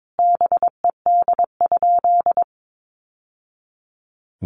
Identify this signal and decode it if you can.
Morse code, heard as a keyed tone, about 22 wpm, 700 Hz. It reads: BED?
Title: Sentences from Top 1200 Words 22wpm